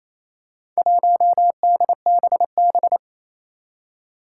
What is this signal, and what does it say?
Morse code, heard as a keyed tone, about 28 wpm, 700 Hz. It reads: 1DBB
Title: Four Challenging Letter-Number Combinations 28wpm